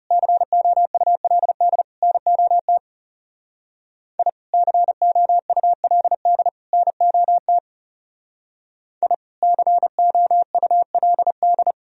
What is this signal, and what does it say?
Morse code, heard as a keyed tone, about 39 wpm, 700 Hz. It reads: COULD NOT I COULD NOT I COULD